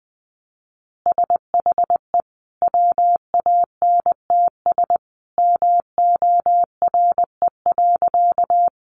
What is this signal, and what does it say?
Morse code, heard as a keyed tone, about 20 wpm, 700 Hz. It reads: SHE WANTS MORE.
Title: Easy Phrases 20wpm